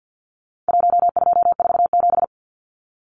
Morse code, heard as a keyed tone, about 50 wpm, 700 Hz. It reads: 1247
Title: Numbers - 4-Digit 50wpm